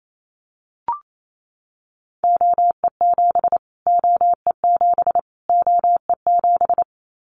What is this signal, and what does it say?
Morse code, heard as a keyed tone, about 28 wpm, 700 Hz. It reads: OE7 OE7 OE7
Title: Callsign Prefixes for Top 25 DXCC - Repeated 3x - 28wpm